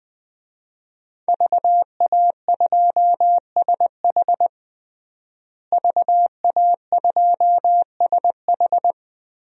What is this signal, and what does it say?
Morse code, heard as a keyed tone, about 20 wpm, 700 Hz. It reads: VA2SH VA2SH
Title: DXCC-100 Call Signs - 5 Characters - Repeated 3x - 20wpm